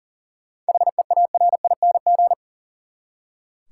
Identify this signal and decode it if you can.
Morse code, heard as a keyed tone, about 40 wpm, 700 Hz. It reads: HEARING
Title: Seven Letter Words - 40wpm